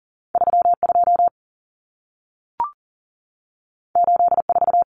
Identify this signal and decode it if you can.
Morse code, heard as a keyed tone, about 40 wpm, 700 Hz. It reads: 32 84
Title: Numbers - 2-Digit 40wpm